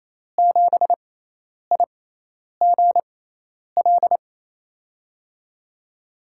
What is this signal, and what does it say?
Morse code, heard as a keyed tone, about 28 wpm, spaced as for 10 wpm, 700 Hz. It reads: 7IGL
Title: USA Call Signs - 6x Character Spacing - 28wpm